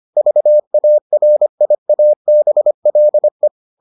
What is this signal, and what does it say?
Morse code, heard as a keyed tone, about 25 wpm, 600 Hz. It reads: VARIABLE